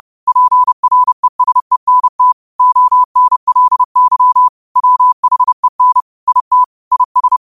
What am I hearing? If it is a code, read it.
Morse code, heard as a keyed tone, about 30 wpm, 1000 Hz. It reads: PRESENT ONLY WHEN IT IS